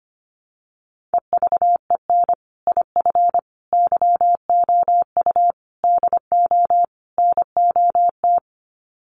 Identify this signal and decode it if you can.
Morse code, heard as a keyed tone, about 25 wpm, 700 Hz. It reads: EVEN IF YOU DO NOT
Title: Sets of 5 Words Encore 25wpm